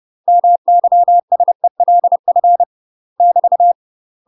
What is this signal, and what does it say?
Morse code, heard as a keyed tone, about 30 wpm, 700 Hz. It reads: MYSELF <BT>
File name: UK MASTER QSOs TRAINING / 30WPM / UK MASTER QSOs 0001